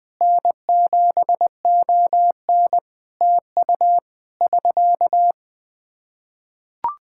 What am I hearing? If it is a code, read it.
Morse code, heard as a keyed tone, about 21 wpm, 700 Hz. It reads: N7ON TU <SK>
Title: Practice QSO - Exchange 4 - Ending - 20wpm